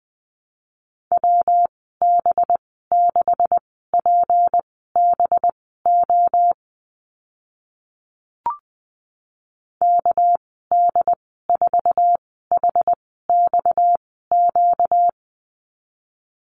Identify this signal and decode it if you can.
Morse code, heard as a keyed tone, about 20 wpm, spaced as for 14 wpm, 700 Hz. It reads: WB6PBO KD4HXQ
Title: 245 Lesson 39g - Letter z - Call Signs 20wpm